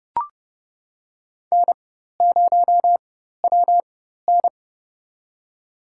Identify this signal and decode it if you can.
Morse code, heard as a keyed tone, about 30 wpm, spaced as for 14 wpm, 700 Hz. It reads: N0WN
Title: USA Call Signs - 4 Characters - 4x Character Spacing - 30wpm